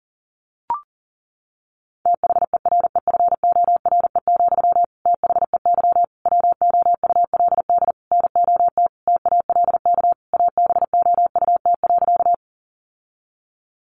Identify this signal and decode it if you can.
Morse code, heard as a keyed tone, about 40 wpm, 700 Hz. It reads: THEREFORE, THEY WOULD NOT TALK ABOUT.
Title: Sentences from Top 800 Words 40wpm